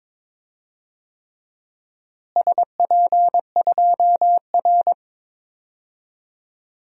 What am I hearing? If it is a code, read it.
Morse code, heard as a keyed tone, about 22 wpm, 700 Hz. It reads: SP2R
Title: DXCC-100 Call Signs - 4 Characters - Repeated 3x - 22wpm